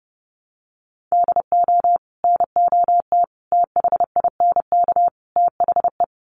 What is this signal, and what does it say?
Morse code, heard as a keyed tone, about 30 wpm, 700 Hz. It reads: DO NOT THINK THE
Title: Sets of 4 Words 30wpm